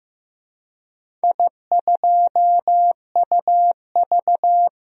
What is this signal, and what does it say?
Morse code, heard as a keyed tone, about 15 wpm, 700 Hz. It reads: I2UV
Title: DXCC-100 Call Signs - 4 Characters - 15wpm